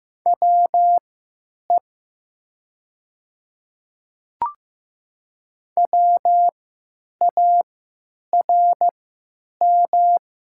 Morse code, heard as a keyed tone, about 15 wpm, spaced as for 9 wpm, 700 Hz. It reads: WE WARM
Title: 3x Character Spacing - Top 500 Words - 15wpm